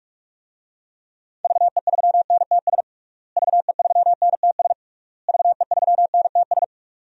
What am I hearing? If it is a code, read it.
Morse code, heard as a keyed tone, about 41 wpm, 700 Hz. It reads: VE3NTS VE3NTS VE3NTS
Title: DXCC-100 Call Signs - 6 Characters - Repeated 3x - 45wpm